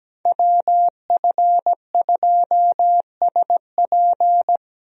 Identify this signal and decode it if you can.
Morse code, heard as a keyed tone, about 17 wpm, 700 Hz. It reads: WF2SP
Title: USA Call Signs - 5 Characters - Repeated 3x - 17wpm